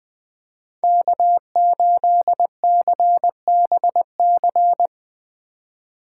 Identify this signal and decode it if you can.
Morse code, heard as a keyed tone, about 20 wpm, 700 Hz. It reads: K8CBC
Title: Speed Racing - USA Call Signs 17wpm